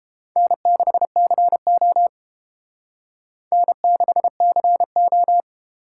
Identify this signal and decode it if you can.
Morse code, heard as a keyed tone, about 30 wpm, 700 Hz. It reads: N6CO N6CO